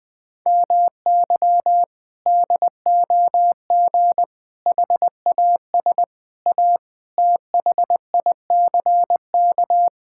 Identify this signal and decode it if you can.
Morse code, heard as a keyed tone, about 20 wpm, 700 Hz. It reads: MY DOG HAS A THICK
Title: Sentences from Top 1600 Words 20wpm